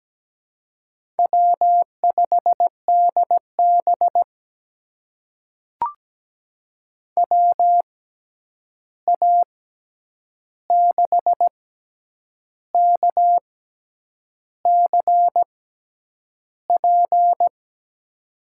Morse code, heard as a keyed tone, about 17 wpm, spaced as for 8 wpm, 700 Hz. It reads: W5DB WA6KCP
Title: USA Call Signs - 6x Character Spacing - 17wpm